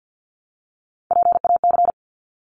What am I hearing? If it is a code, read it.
Morse code, heard as a keyed tone, about 48 wpm, 700 Hz. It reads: PAC
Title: ARRL Sections 50wpm